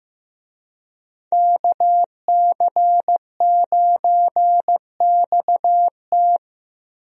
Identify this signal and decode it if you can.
Morse code, heard as a keyed tone, about 15 wpm, 700 Hz. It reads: KC9XT
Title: USA Call Signs - 5 Characters - Repeated 2x - 15wpm